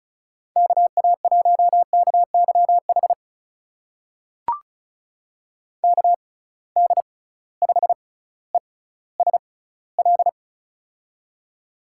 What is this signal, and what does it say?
Morse code, heard as a keyed tone, about 36 wpm, spaced as for 18 wpm, 700 Hz. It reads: KA1KYH KD5ESL
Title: USA Call Signs - 6 Characters - 6x Character Spacing - 35wpm